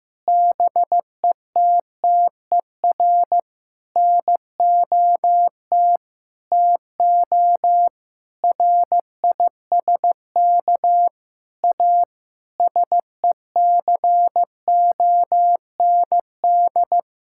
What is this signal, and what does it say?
Morse code, heard as a keyed tone, about 16 wpm, 700 Hz. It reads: BETTER NOT TO RISK A SECOND
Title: Sentences from Top 1600 Words 15wpm